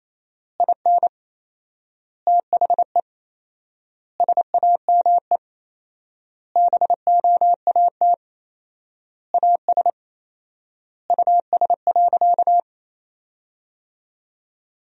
Morse code, heard as a keyed tone, about 28 wpm, 700 Hz. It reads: IN THE SAME BOAT AS US.
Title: 4x Word-Spacing - Sentences from Top 500 Words - 28wpm